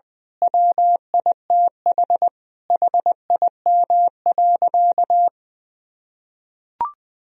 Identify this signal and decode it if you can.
Morse code, heard as a keyed tone, about 21 wpm, 700 Hz. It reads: WITH HIM.
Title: Sentences from Top 1000 Words - Review Part 4 - 20wpm